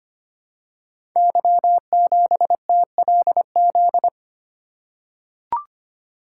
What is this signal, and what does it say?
Morse code, heard as a keyed tone, about 25 wpm, 700 Hz. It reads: Y7TLZ
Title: Code Groups - 5 Letter:Numbers - 25wpm